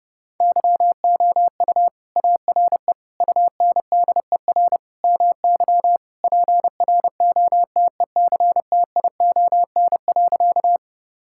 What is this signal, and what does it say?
Morse code, heard as a keyed tone, about 30 wpm, 700 Hz. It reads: YOU ARE UNDER MY PROTECTION.
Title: Sentences from Top 1600 Words 30wpm